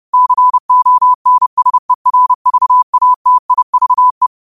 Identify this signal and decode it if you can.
Morse code, heard as a keyed tone, about 30 wpm, 1000 Hz. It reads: CONSERVATIVE
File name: qotdfiles / 30wpm